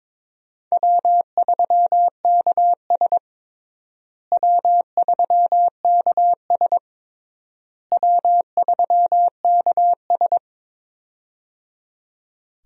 Morse code, heard as a keyed tone, about 22 wpm, 700 Hz. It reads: W3KS W3KS W3KS
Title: USA Call Signs - 4 Characters - Repeated 3x - 22wpm